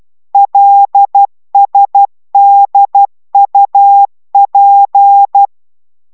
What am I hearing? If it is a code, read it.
Morse code, heard as a keyed tone, about 12 wpm, 800 Hz. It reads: LSDUP